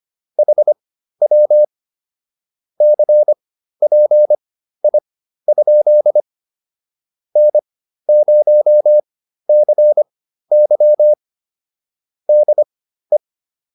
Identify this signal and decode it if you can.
Morse code, heard as a keyed tone, about 25 wpm, spaced as for 13 wpm, 600 Hz. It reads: HW CPI? N0CY DE
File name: QSO310 13